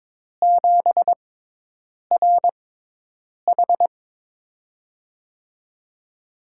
Morse code, heard as a keyed tone, about 21 wpm, spaced as for 8 wpm, 700 Hz. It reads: 7RH
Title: USA Call Signs - 5 Characters - 6x Character Spacing - 22wpm